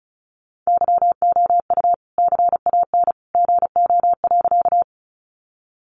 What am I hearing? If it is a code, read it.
Morse code, heard as a keyed tone, about 34 wpm, 700 Hz. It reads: YOU CAN GO.
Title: Sentences from Top 100 Words 35wpm